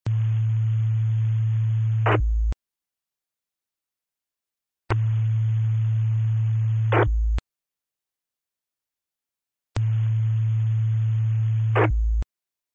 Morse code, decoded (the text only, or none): none